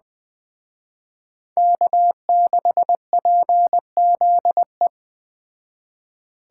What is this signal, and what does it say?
Morse code, heard as a keyed tone, about 21 wpm, 700 Hz. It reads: K6PZE